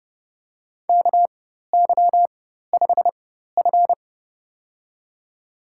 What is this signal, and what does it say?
Morse code, heard as a keyed tone, about 31 wpm, spaced as for 14 wpm, 700 Hz. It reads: KY5F
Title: USA Call Signs - 4x Character Spacing - 30wpm